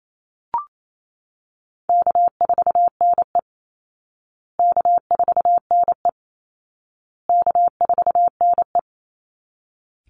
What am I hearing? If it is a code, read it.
Morse code, heard as a keyed tone, about 28 wpm, 700 Hz. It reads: K4NE K4NE K4NE
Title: Easy Call Signs Repeated 3x - 28wpm